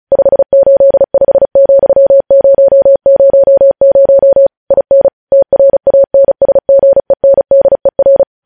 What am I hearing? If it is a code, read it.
Morse code, heard as a keyed tone, about 36 wpm, 550 Hz. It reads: 585,000 IN TRANSGENDER
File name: podcast / News headlines-1x-35x35